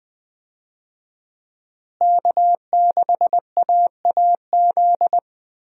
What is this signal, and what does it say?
Morse code, heard as a keyed tone, about 20 wpm, 700 Hz. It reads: K6AAZ
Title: USA Call Signs - 5 Characters - Repeated 3x - 20wpm